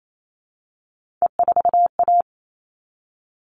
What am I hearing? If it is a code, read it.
Morse code, heard as a keyed tone, about 28 wpm, 700 Hz. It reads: E4A
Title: All Cut Numbers - 3-Digit 28wpm